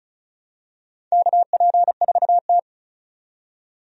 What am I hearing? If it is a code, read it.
Morse code, heard as a keyed tone, about 34 wpm, 700 Hz. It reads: KP4T
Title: DXCC-100 Call Signs - 4 Characters - 35wpm